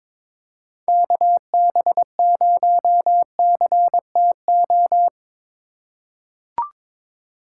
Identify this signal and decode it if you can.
Morse code, heard as a keyed tone, about 22 wpm, 700 Hz. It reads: KB0CTO
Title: USA Call Signs - 6 Characters - 4x Character Spacing - 22wpm